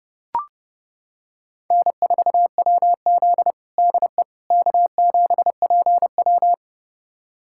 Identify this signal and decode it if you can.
Morse code, heard as a keyed tone, about 30 wpm, 700 Hz. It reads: N4WZ DE K7PW
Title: USA Callsign DE Callsign - 4 Characters - 30wpm